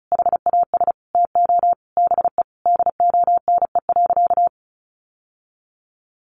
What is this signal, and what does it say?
Morse code, heard as a keyed tone, about 34 wpm, 700 Hz. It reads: HAS TO BE DONE.
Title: Sentences from Top 700 Words 35wpm